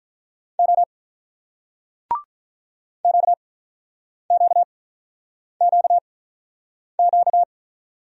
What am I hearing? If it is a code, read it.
Morse code, heard as a keyed tone, about 48 wpm, spaced as for 9 wpm, 690 Hz. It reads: O QQQQ